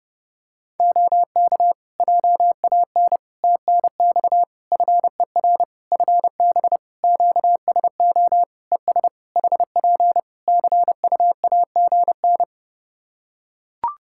Morse code, heard as a keyed tone, about 30 wpm, 700 Hz. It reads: OK JAN TNX FER FB QSO ES HP CUAGN
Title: K1USN SST QSO Practice 30wpm